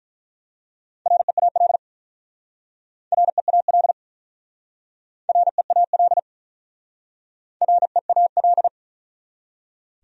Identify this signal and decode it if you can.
Morse code, heard as a keyed tone, about 43 wpm, 700 Hz. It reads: REAL REAL REAL REAL
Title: Speed-Racing Top 500 Words 35wpm